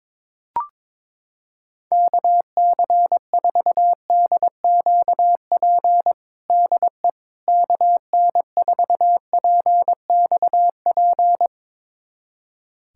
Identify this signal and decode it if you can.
Morse code, heard as a keyed tone, about 22 wpm, 700 Hz. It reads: KC4DQP DE KN4PXP
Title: USA Callsign DE Callsign 22wpm